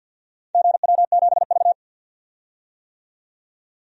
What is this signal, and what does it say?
Morse code, heard as a keyed tone, about 51 wpm, 700 Hz. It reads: GW7V
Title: DXCC-100 Call Signs - 4 Characters - 50wpm